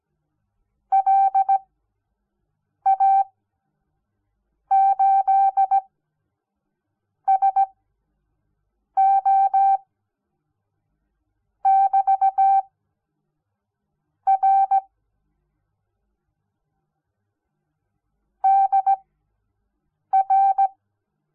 Morse code, heard as a keyed tone, about 17 wpm, spaced as for 6 wpm, 780 Hz. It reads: LA8SO <BT> R DR
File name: Onlinelehrgang / morsen / DJ4UF Morsen06